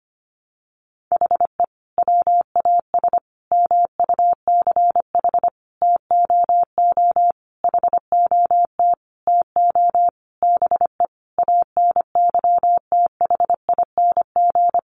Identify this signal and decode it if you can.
Morse code, heard as a keyed tone, about 25 wpm, 700 Hz. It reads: HE WAS MUCH TOO HOT TO BE ANYTHING
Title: Sentences from Top 1200 Words 25wpm